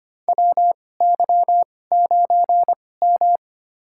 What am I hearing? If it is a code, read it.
Morse code, heard as a keyed tone, about 25 wpm, spaced as for 18 wpm, 700 Hz. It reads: WY9M
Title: USA Call Signs - 2x Character Spacing - 25wpm